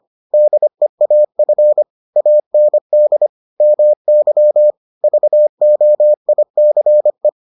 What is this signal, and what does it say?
Morse code, heard as a keyed tone, about 25 wpm, 600 Hz. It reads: DEAF AND MY VOICE